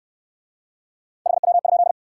Morse code, heard as a keyed tone, about 65 wpm, 700 Hz. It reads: HW?